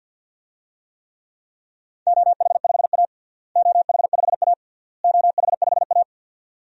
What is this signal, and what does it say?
Morse code, heard as a keyed tone, about 51 wpm, 700 Hz. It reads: OH5A OH5A OH5A
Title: DXCC-100 Call Signs - 4 Characters - Repeated 3x - 50wpm